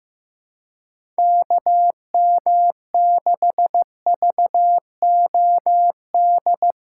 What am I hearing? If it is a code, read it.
Morse code, heard as a keyed tone, about 16 wpm, 700 Hz. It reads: KM6VOD
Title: USA Callsign DE Callsign 15wpm